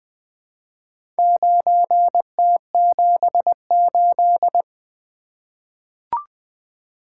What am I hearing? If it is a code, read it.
Morse code, heard as a keyed tone, about 20 wpm, 700 Hz. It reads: 9T78
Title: Four Letter-Number Combinations - 20wpm